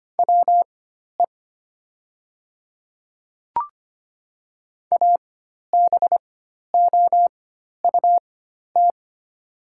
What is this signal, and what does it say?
Morse code, heard as a keyed tone, about 25 wpm, spaced as for 12 wpm, 700 Hz. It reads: WE ABOUT
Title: 4x Character Spacing - Top 100 Words - 25wpm